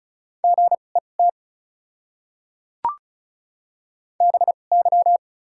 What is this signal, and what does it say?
Morse code, heard as a keyed tone, about 36 wpm, spaced as for 25 wpm, 700 Hz. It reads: GET BY
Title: 2x Character Spacing - Top 500 Words - 35wpm